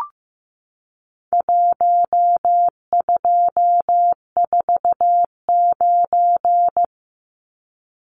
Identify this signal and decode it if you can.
Morse code, heard as a keyed tone, about 15 wpm, 700 Hz. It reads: 1249
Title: Numbers - 4-Digit 15wpm